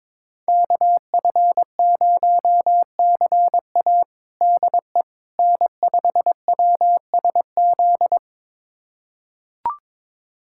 Morse code, heard as a keyed tone, about 22 wpm, 700 Hz. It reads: KF0CA DE N5WSZ
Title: USA Callsign DE Callsign - 5 Characters - 22wpm